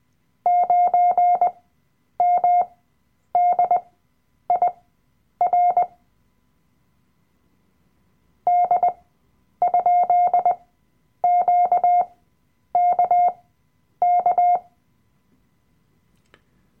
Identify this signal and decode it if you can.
Morse code, heard as a keyed tone, about 20 wpm, spaced as for 9 wpm, 710 Hz. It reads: 9MDIR D?QKK
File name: Workshop Dateien / CW-Online-Kurs / Lektion 007